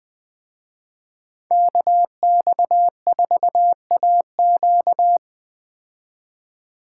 Morse code, heard as a keyed tone, about 20 wpm, 700 Hz. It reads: KX4AQ